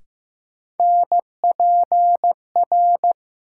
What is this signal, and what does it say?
Morse code, heard as a keyed tone, about 15 wpm, 700 Hz. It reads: NPR
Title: Sets of 5 Words - New Word Drill - 15wpm